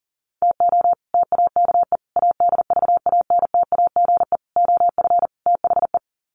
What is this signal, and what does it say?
Morse code, heard as a keyed tone, about 40 wpm, 700 Hz. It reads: TO TAKE ADVANTAGE OF THE
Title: Sets of 5 Words 40wpm